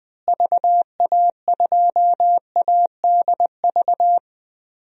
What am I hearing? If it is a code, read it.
Morse code, heard as a keyed tone, about 20 wpm, 700 Hz. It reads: VA2ADV
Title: DXCC-100 Call Signs - 6 Characters - Repeated 3x - 20wpm